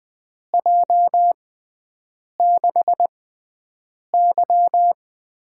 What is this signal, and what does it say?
Morse code, heard as a keyed tone, about 20 wpm, spaced as for 7 wpm, 700 Hz. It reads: J6Y
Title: USA Call Signs - 6 Characters - 6x Character Spacing - 20wpm